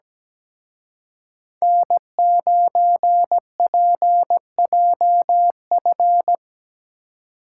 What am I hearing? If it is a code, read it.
Morse code, heard as a keyed tone, about 20 wpm, 700 Hz. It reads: N9PJF
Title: Speed Racing - USA 1x3 Call Signs 17wpm